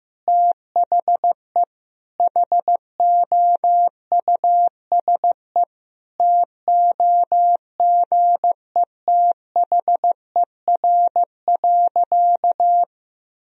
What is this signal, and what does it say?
Morse code, heard as a keyed tone, about 15 wpm, 700 Hz. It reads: THE HOUSE TOGETHER.
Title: Sentences from Top 1000 Words - Review Part 1 - 15wpm